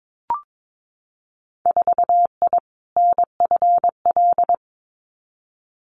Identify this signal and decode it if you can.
Morse code, heard as a keyed tone, about 22 wpm, 700 Hz. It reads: 4I NFL
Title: Winter Field Day Exchanges 22wpm